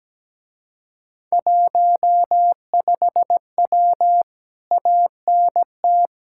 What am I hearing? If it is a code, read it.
Morse code, heard as a keyed tone, about 17 wpm, 700 Hz. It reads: 15W ANT
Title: Speed-Racing Common QSO Phrases - 17wpm